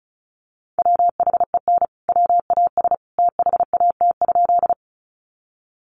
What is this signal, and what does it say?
Morse code, heard as a keyed tone, about 34 wpm, 700 Hz. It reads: WHEN WAS THAT?
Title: Sentences from Top 100 Words 35wpm